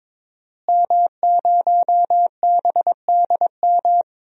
Morse code, heard as a keyed tone, about 22 wpm, 700 Hz. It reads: M0BDM
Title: DXCC-100 Call Signs - 5 Characters - Repeated 3x - 22wpm